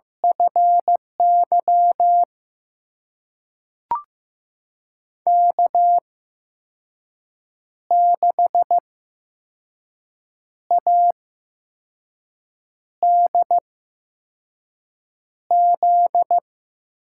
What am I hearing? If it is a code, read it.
Morse code, heard as a keyed tone, about 15 wpm, spaced as for 5 wpm, 700 Hz. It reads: FY K6ADZ
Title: USA Call Signs - 8x Character Spacing - 15wpm